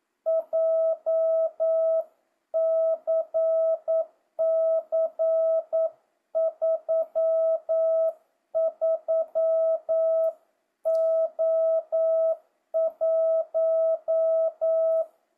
Morse code, heard as a keyed tone, about 9 wpm, 650 Hz. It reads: JCC 33O1
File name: jh8cbh / 2025cw